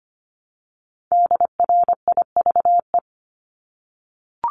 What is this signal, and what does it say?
Morse code, heard as a keyed tone, about 25 wpm, 700 Hz. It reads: DRIVE
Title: Multi-letter Phonemes Repeated 3x 25wpm